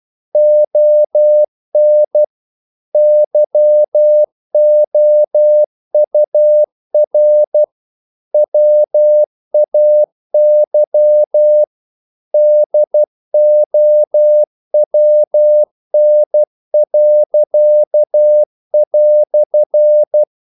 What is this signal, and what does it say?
Morse code, heard as a keyed tone, about 12 wpm, 600 Hz. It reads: ON YOUR WAY DOWN."
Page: English quotations in CW (telegraphy - morse)